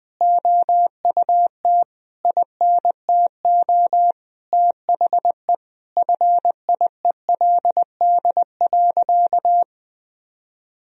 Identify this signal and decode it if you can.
Morse code, heard as a keyed tone, about 20 wpm, 700 Hz. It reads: OUT INTO THE FIELD.
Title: Sentences from Top 1000 Words - Review Part 2 - 20wpm